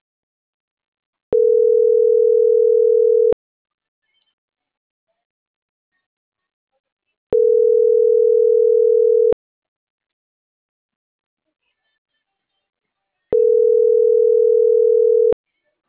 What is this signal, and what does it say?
Ringback tone (North America), 440 + 480 Hz.